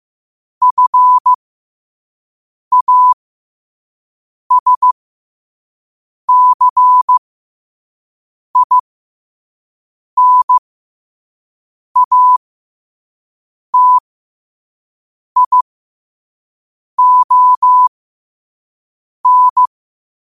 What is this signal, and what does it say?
Morse code, heard as a keyed tone, about 15 wpm, spaced as for 5 wpm, 1000 Hz. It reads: FASCINATION
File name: qotdfiles / 5wpm